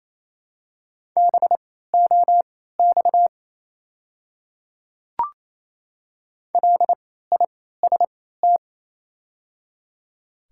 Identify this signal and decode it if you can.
Morse code, heard as a keyed tone, about 28 wpm, spaced as for 16 wpm, 700 Hz. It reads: BOX LIST
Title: 3x Character Spacing - Top 500 Words - 28wpm